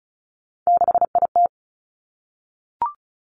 Morse code, heard as a keyed tone, about 36 wpm, 700 Hz. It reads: 6IT
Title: Three Challenging Letter-Number Combinations 35wpm